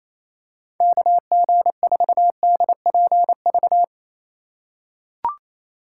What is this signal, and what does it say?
Morse code, heard as a keyed tone, about 28 wpm, 700 Hz. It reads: KG4DPV
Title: USA Call Signs - 6 Characters - 3x Character Spacing - 28wpm